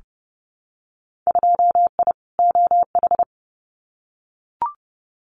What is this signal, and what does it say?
Morse code, heard as a keyed tone, about 30 wpm, 700 Hz. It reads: 2I OH
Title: Winter Field Day Exchanges 30wpm